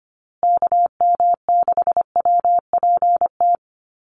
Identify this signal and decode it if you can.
Morse code, heard as a keyed tone, about 25 wpm, 700 Hz. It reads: KM6WPT
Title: USA Call Signs Repeated 3x 25wpm